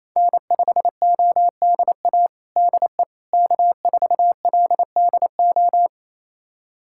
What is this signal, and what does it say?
Morse code, heard as a keyed tone, about 27 wpm, 700 Hz. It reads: N5ODA DE K4LDO
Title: USA Callsign DE Callsign - 5 Characters - 28wpm